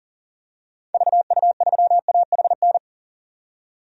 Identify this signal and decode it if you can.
Morse code, heard as a keyed tone, about 40 wpm, 700 Hz. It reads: VU3AHN
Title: DXCC-100 Call Signs - 6 Characters - 40wpm